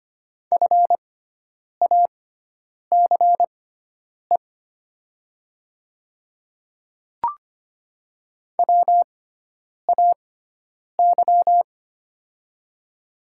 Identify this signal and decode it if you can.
Morse code, heard as a keyed tone, about 25 wpm, spaced as for 9 wpm, 700 Hz. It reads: FACE WAY
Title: 6x Character Spacing - Top 500 Words - 25wpm